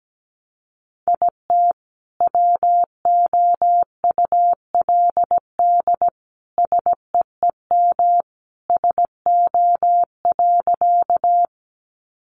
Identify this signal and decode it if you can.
Morse code, heard as a keyed tone, about 17 wpm, 700 Hz. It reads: IT WOULD SEEM SO.
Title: Sentences from Top 300 Words 17wpm